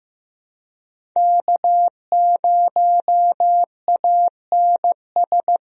Morse code, heard as a keyed tone, about 16 wpm, 700 Hz. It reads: K0ANS
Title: USA Call Signs - 5 Characters - 15wpm